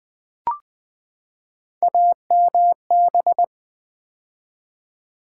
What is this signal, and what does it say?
Morse code, heard as a keyed tone, about 20 wpm, 700 Hz. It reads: AMB
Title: Build A Word - 3 Letters At A Time - 9-Letter Words - 20wpm